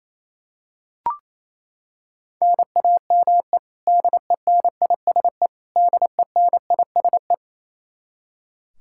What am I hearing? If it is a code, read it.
Morse code, heard as a keyed tone, about 28 wpm, 700 Hz. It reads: NAME DENISE DENISE
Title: Practice QSO - Exchange 1 - Beginning - 28wpm